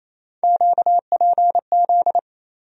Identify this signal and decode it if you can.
Morse code, heard as a keyed tone, about 28 wpm, 700 Hz. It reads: QPZ
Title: USA Call Sign Postfixes - 3 Letters 28wpm